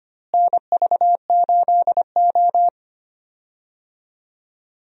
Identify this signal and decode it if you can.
Morse code, heard as a keyed tone, about 25 wpm, 700 Hz. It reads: NV8O
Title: USA Call Signs - 4 Characters - Repeated 3x - 25wpm